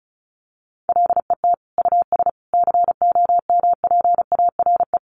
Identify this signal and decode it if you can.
Morse code, heard as a keyed tone, about 34 wpm, 700 Hz. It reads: LET US COMPARE
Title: Sentences from Top 1300 Words 35wpm